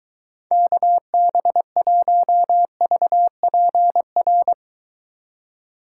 Morse code, heard as a keyed tone, about 23 wpm, 700 Hz. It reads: KB1VPR